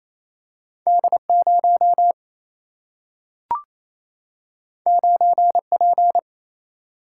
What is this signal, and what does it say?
Morse code, heard as a keyed tone, about 28 wpm, 700 Hz. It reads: D 0 9 P
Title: Single-Letter Single-Number 28wpm